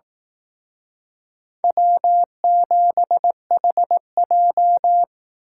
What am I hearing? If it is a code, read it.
Morse code, heard as a keyed tone, about 18 wpm, 700 Hz. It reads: W7HJ
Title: Speed Racing - USA Call Signs 15wpm